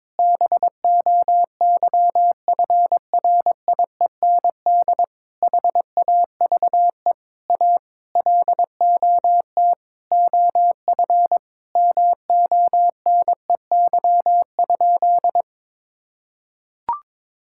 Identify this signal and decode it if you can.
Morse code, heard as a keyed tone, about 22 wpm, 700 Hz. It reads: BOYFRIEND HAVE A LOT OF MONEY?
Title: Sentences from Top 1000 Words - Review Part 5 - 22wpm